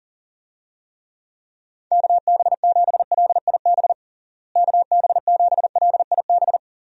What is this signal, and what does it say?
Morse code, heard as a keyed tone, about 40 wpm, 700 Hz. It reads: KB7LIB KB7LIB
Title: USA Call Signs - 6 Characters - Repeated 2x - 40wpm